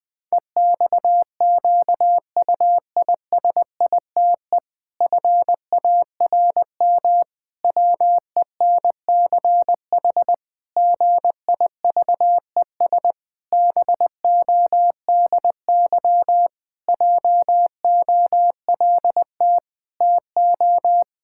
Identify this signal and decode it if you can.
Morse code, heard as a keyed tone, about 21 wpm, 700 Hz. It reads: EXQUISITE FARM WENCH GIVES BODY JOLT TO
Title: Pangram Challenge v2 - 20wpm